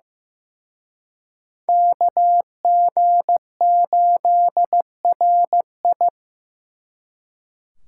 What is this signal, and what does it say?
Morse code, heard as a keyed tone, about 18 wpm, 700 Hz. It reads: KG8RI
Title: Speed Racing - USA Call Signs 15wpm